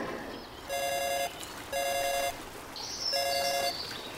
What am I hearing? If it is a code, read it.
There is no code.